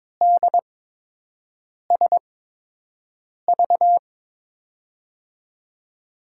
Morse code, heard as a keyed tone, about 22 wpm, spaced as for 6 wpm, 700 Hz. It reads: DSV
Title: USA Call Signs - 6 Characters - 8x Character Spacing - 22wpm